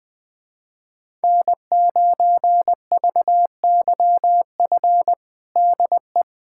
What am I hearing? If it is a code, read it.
Morse code, heard as a keyed tone, about 20 wpm, 700 Hz. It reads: N9VYF DE
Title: USA Callsign DE Callsign - 5 Characters - 20wpm